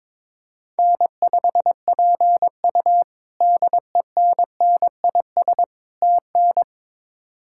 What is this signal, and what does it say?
Morse code, heard as a keyed tone, about 22 wpm, 700 Hz. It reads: N5PU DENNIS TN
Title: K1USN SST QSO Practice - 22wpm